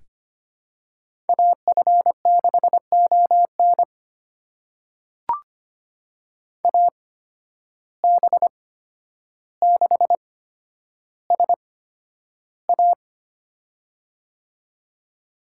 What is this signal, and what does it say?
Morse code, heard as a keyed tone, about 25 wpm, spaced as for 11 wpm, 700 Hz. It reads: AF6ON AB6SA
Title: USA Call Signs - 5 Characters - 8x Character Spacing - 25wpm